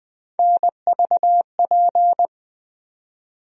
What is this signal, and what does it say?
Morse code, heard as a keyed tone, about 20 wpm, 700 Hz. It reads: NVP
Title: USA Call Sign Postfixes - 3 Letters 20wpm